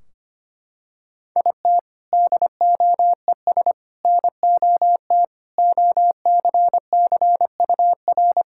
Morse code, heard as a keyed tone, about 25 wpm, 700 Hz. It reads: IT DOES NOT OCCUR
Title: Sentences from Top 1600 Words 25wpm